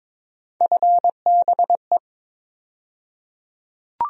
Morse code, heard as a keyed tone, about 22 wpm, 700 Hz. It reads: FBE
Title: USA Call Sign Postfixes - 3 Letters 22wpm